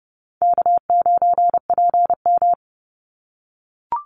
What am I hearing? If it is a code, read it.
Morse code, heard as a keyed tone, about 30 wpm, 700 Hz. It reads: K9PM
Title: NA SOTA Chaser Callsigns 30wpm